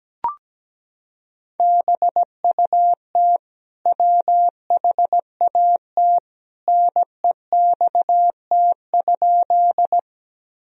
Morse code, heard as a keyed tone, about 17 wpm, 700 Hz. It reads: BUT WHAT NEXT?
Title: Sentences from Top 1000 Words - Review Part 1 - 17wpm